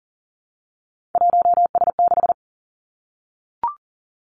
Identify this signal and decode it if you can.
Morse code, heard as a keyed tone, about 40 wpm, 700 Hz. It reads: 1S6
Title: Three Challenging Letter-Number Combinations 40wpm